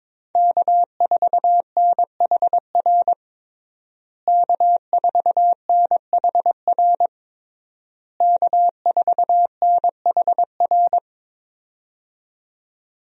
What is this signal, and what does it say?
Morse code, heard as a keyed tone, about 22 wpm, 700 Hz. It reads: K4NHR K4NHR K4NHR
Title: USA Call Signs - 5 Characters - Repeated 3x - 22wpm